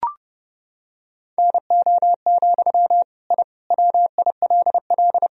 Morse code, heard as a keyed tone, about 30 wpm, 700 Hz. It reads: NO, I WILL
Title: Sentences from Top 1000 Words - Review Part 1 - 30wpm